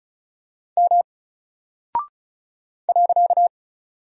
Morse code, heard as a keyed tone, about 36 wpm, 700 Hz. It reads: M .
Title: Single Character (Letter:Number:Punctuation:Prosign) Alt 35wpm